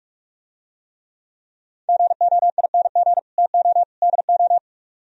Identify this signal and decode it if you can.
Morse code, heard as a keyed tone, about 49 wpm, 700 Hz. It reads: GOING TO DO